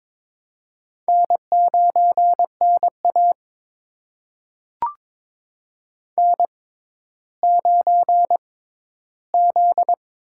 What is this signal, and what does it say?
Morse code, heard as a keyed tone, about 22 wpm, spaced as for 13 wpm, 700 Hz. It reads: N9NA N9Z
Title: USA Call Signs - 4 Characters - 6x Character Spacing - 22wpm